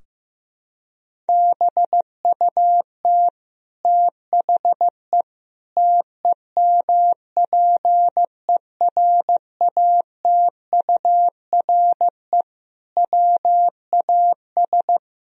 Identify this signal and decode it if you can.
Morse code, heard as a keyed tone, about 15 wpm, 700 Hz. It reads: BUT THE TEMPERATURE WAS
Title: Sentences from Top 1000 Words - Review Part 4 - 15wpm